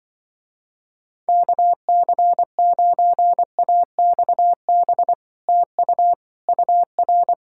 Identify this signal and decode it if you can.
Morse code, heard as a keyed tone, about 25 wpm, 700 Hz. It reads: KC9AXB TU UR
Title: Speed-Racing Common QSO Phrases - 20wpm